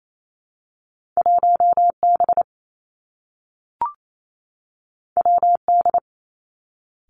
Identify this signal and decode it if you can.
Morse code, heard as a keyed tone, about 28 wpm, 700 Hz. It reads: 1 B W D
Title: Two Challenging Letter-Number Combinations 28wpm